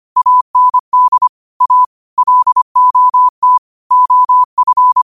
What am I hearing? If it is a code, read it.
Morse code, heard as a keyed tone, about 25 wpm, 1000 Hz. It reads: AND A LOT OF